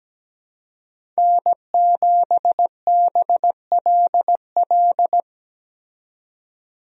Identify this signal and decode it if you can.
Morse code, heard as a keyed tone, about 17 wpm, 700 Hz. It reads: N7BLL
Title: USA Call Signs - 5 Characters - 17wpm